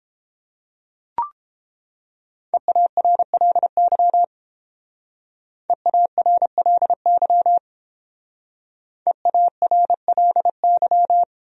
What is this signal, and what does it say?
Morse code, heard as a keyed tone, about 29 wpm, 700 Hz. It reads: EARLY EARLY EARLY